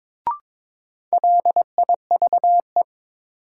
Morse code, heard as a keyed tone, about 22 wpm, 700 Hz. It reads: LIVE
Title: Sentences from Top 200 Words - New Word Drill - 22wpm